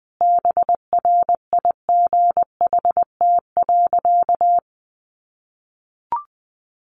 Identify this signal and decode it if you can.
Morse code, heard as a keyed tone, about 20 wpm, 700 Hz. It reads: BRIGHT.
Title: Sentences from Top 1500 Words 20wpm